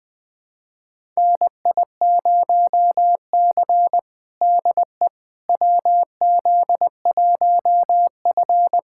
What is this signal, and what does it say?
Morse code, heard as a keyed tone, about 20 wpm, 700 Hz. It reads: NI0C DE WZ1F
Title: USA Callsign DE Callsign - 4 Characters - 20wpm